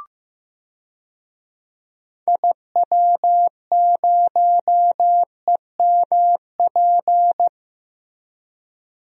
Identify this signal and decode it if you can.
Morse code, heard as a keyed tone, about 15 wpm, 700 Hz. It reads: IW0EMP
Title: DXCC-100 Call Signs - 6 Characters - Repeated 3x - 15wpm